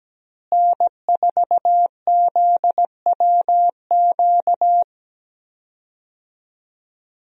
Morse code, heard as a keyed tone, about 17 wpm, 700 Hz. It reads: N4ZWQ
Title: USA Call Signs - 5 Characters - Repeated 3x - 17wpm